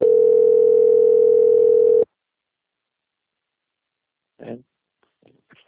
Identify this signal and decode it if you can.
Ringback tone (North America), 440 + 480 Hz.
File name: out- - - - - .6809